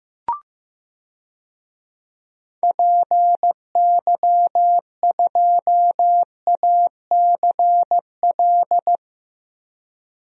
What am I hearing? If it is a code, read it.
Morse code, heard as a keyed tone, about 15 wpm, 700 Hz. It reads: PY2ACL
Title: DXCC-100 Call Signs - 6 Characters - 15wpm